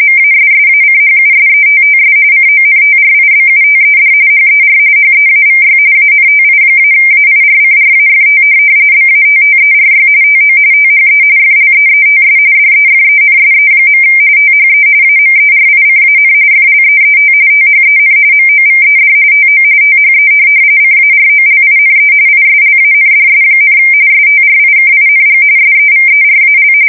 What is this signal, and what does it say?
RTTY, about 45.45 baud. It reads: MASSACHUSETTS CONTRIBUTED HER HEAVY ANTI-AIRCRAFT BATTERY TO THE FLEETS DEFENSE. THE AMERICAN FLEET THEN MOVED TO THE CAROLINE ISLANDS, STRIKING SEVERAL
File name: fsk170 / USS-Massachusetts